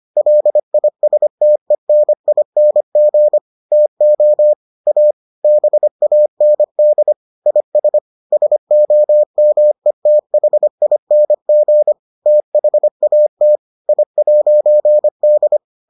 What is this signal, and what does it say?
Morse code, heard as a keyed tone, about 25 wpm, 600 Hz. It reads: LISTENING TO A BAND IS SOMETHING THAT I'D